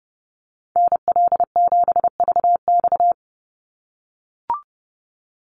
Morse code, heard as a keyed tone, about 28 wpm, 700 Hz. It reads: NL7VX
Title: Call Signs - 30wpm